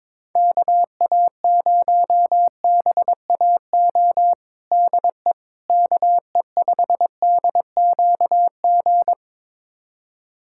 Morse code, heard as a keyed tone, about 22 wpm, 700 Hz. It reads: KA0BAO DE KE5DQG
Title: USA Callsign DE Callsign 22wpm